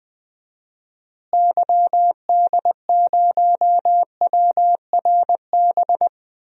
Morse code, heard as a keyed tone, about 20 wpm, 700 Hz. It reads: YD0WRB
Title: DXCC-100 Call Signs - 6 Characters - Repeated 3x - 20wpm